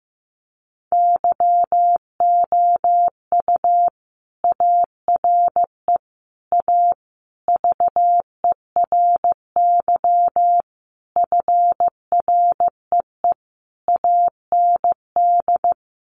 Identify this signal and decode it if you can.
Morse code, heard as a keyed tone, about 15 wpm, 700 Hz. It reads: YOU ARE A VERY FREE AND
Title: Sentences from Top 1000 Words 15wpm